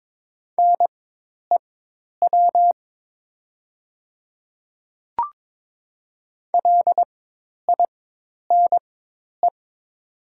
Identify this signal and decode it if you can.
Morse code, heard as a keyed tone, about 22 wpm, spaced as for 10 wpm, 700 Hz. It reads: NEW LINE
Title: 4x Character Spacing - Top 200 Words - 22wpm